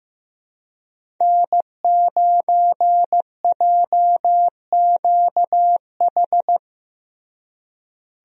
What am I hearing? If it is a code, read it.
Morse code, heard as a keyed tone, about 15 wpm, 700 Hz. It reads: N9JQH
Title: USA Call Signs - 5 Characters - Repeated 2x - 15wpm